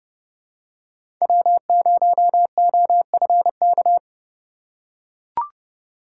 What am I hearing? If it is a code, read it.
Morse code, heard as a keyed tone, about 29 wpm, 700 Hz. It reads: W0OFK
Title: USA Call Signs - 6x Character Spacing - 30wpm